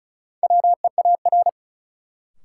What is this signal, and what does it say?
Morse code, heard as a keyed tone, about 34 wpm, 700 Hz. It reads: WEAR
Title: Sentences from Top 1500 Words - New Word Drill Alt - 35wpm